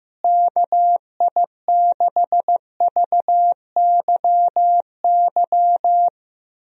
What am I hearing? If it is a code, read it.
Morse code, heard as a keyed tone, about 16 wpm, 700 Hz. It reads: KI6VYY
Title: USA Call Signs - 6 Characters - Repeated 3x - 15wpm